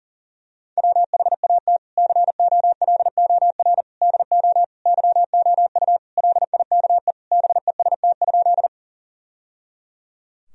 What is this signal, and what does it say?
Morse code, heard as a keyed tone, about 40 wpm, 700 Hz. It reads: WHAT COLOR DO YOU LIKE BEST?
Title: Sentences from Top 1000 Words - Review Part 2 - 40wpm